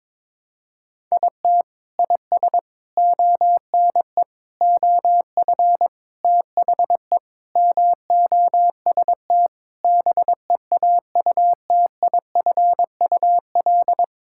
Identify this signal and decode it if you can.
Morse code, heard as a keyed tone, about 22 wpm, 700 Hz. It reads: IT IS ONE OF THE MOST BEAUTIFUL
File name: practice / 22wpm / Sentences with 4-Word Phrases - 22wpm